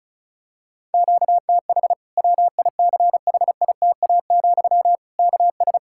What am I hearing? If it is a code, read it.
Morse code, heard as a keyed tone, about 36 wpm, 700 Hz. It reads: QTH WICHITA, KS
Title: Practice QSO - Exchange 1 - Beginning - 35wpm